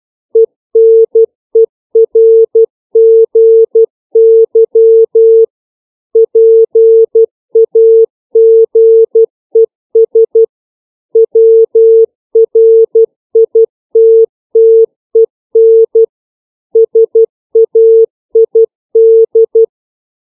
Morse code, heard as a keyed tone, about 12 wpm, 450 Hz. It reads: ENERGY PAGES WRITTEN SAID
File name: CW / five common words 3